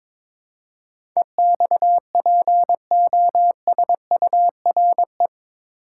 Morse code, heard as a keyed tone, about 22 wpm, 700 Hz. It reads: EXPOSURE
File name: practice / 22wpm / Eight Letter Words - 22wpm